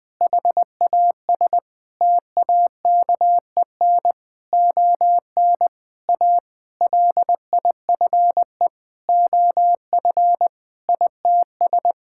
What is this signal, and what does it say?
Morse code, heard as a keyed tone, about 21 wpm, 700 Hz. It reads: HAS TAKEN ON A LIFE OF ITS
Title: Sentences with 8-Word Phrases - 20wpm